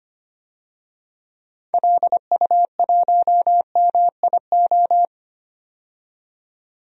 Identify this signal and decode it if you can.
Morse code, heard as a keyed tone, about 25 wpm, 700 Hz. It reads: LU1MIO
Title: DXCC-100 Call Signs - 6 Characters - 25wpm